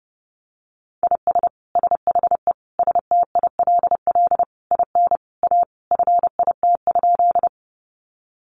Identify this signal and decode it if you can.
Morse code, heard as a keyed tone, about 30 wpm, 700 Hz. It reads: IS SHE STILL IN A FIT?
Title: Sentences from Top 800 Words 30wpm